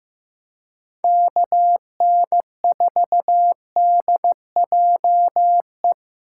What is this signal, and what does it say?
Morse code, heard as a keyed tone, about 15 wpm, 700 Hz. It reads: KN4DJE
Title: USA Call Signs - 6 Characters - 3x Character Spacing - 15wpm